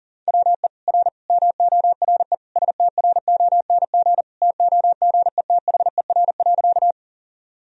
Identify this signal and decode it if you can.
Morse code, heard as a keyed tone, about 40 wpm, 700 Hz. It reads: WE R MORE STRONG TOGETHER.
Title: Sentences from Top 500 Words and Abbreviations 40wpm